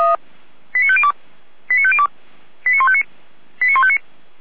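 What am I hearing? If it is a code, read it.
Five-tone selective call (ZVEI), heard as quick runs of five tones, tones of about 70 ms. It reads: 89502 89502 89159 89159
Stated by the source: ZVEI five-tone selective call, address 89108, received on an SDR.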